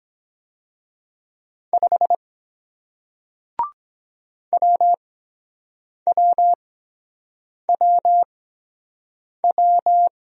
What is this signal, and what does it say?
Morse code, heard as a keyed tone, about 23 wpm, spaced as for 7 wpm, 700 Hz. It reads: 5 WWWW